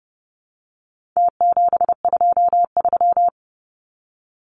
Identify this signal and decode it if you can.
Morse code, heard as a keyed tone, about 30 wpm, 700 Hz. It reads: T723
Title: Numbers - Contest Style - 4-Digit 30wpm